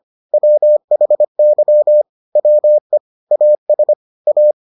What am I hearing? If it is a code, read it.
Morse code, heard as a keyed tone, about 25 wpm, 600 Hz. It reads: WHY WE AS A